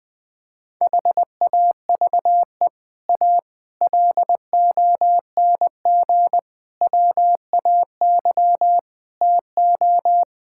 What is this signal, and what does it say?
Morse code, heard as a keyed tone, about 21 wpm, 700 Hz. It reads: HAVE A LONG WAY TO
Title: Sets of 6 Words - 20wpm